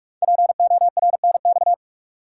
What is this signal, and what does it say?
Morse code, heard as a keyed tone, about 44 wpm, 690 Hz. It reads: PORNX
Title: Code Groups - 5 Letters - 45wpm